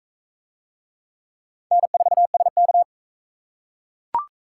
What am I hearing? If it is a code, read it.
Morse code, heard as a keyed tone, about 41 wpm, 700 Hz. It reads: N4SK